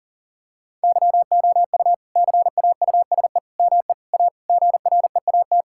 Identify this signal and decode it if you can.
Morse code, heard as a keyed tone, about 40 wpm, 700 Hz. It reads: YOU CAUSE ME A GREAT
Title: Sentences from Top 1600 Words 40wpm